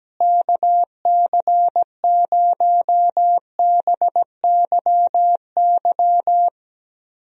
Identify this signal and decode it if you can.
Morse code, heard as a keyed tone, about 17 wpm, 700 Hz. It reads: KC0BYY
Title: USA Call Signs - 6 Characters - Repeated 2x - 17wpm